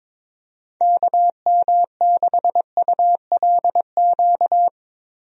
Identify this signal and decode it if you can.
Morse code, heard as a keyed tone, about 21 wpm, 700 Hz. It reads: KM6ULQ
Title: USA Call Signs - 6 Characters - Repeated 3x - 22wpm